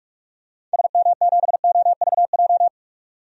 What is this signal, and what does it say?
Morse code, heard as a keyed tone, about 46 wpm, 700 Hz. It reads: SM7OVJ
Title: DXCC-100 Call Signs - 6 Characters - 45wpm